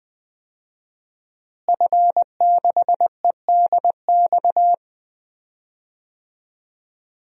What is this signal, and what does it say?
Morse code, heard as a keyed tone, about 20 wpm, 700 Hz. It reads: F6EDX
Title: DXCC-100 Call Signs - 5 Characters - Repeated 3x - 20wpm